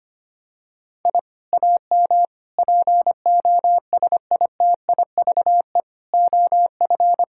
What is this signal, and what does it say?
Morse code, heard as a keyed tone, about 25 wpm, 700 Hz. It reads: I AM POSITIVE OF
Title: Sentences from Top 1000 Words - Review Part 5 - 25wpm